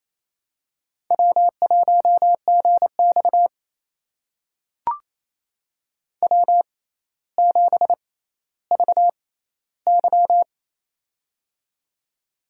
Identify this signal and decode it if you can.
Morse code, heard as a keyed tone, about 28 wpm, spaced as for 14 wpm, 700 Hz. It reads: W1GX W7VY
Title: USA Call Signs - 4 Characters - 6x Character Spacing - 28wpm